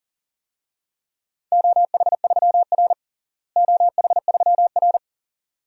Morse code, heard as a keyed tone, about 40 wpm, 700 Hz. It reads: OH3R OH3R
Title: DXCC-100 Call Signs - 4 Characters - Repeated 2x - 40wpm